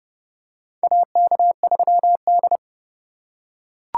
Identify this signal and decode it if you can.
Morse code, heard as a keyed tone, about 30 wpm, 700 Hz. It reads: AK3D
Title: USA Call Signs - 4 Characters - 3x Character Spacing - 30wpm